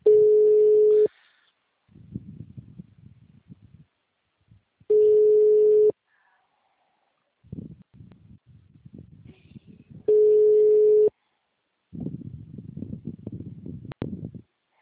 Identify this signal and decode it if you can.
Ringback tone (Europe, 425 Hz).
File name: out- -2024- - - .114